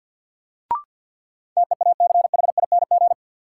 Morse code, heard as a keyed tone, about 51 wpm, 700 Hz. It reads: TEACHING
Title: Sentences from Top 1100 Words - New Word Drill Alt - 50wpm